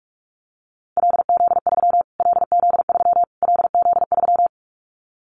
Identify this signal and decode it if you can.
Morse code, heard as a keyed tone, about 44 wpm, 700 Hz. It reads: LZ3 LZ3 LZ3
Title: Callsign Prefixes for Top 25 DXCC - Repeated 3x - 45wpm